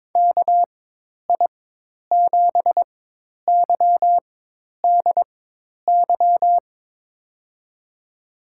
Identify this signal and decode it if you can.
Morse code, heard as a keyed tone, about 22 wpm, spaced as for 10 wpm, 700 Hz. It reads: KI7YDY
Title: USA Call Signs - 6 Characters - 4x Character Spacing - 22wpm